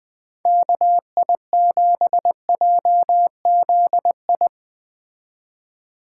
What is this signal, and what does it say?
Morse code, heard as a keyed tone, about 20 wpm, 700 Hz. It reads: KI7JZI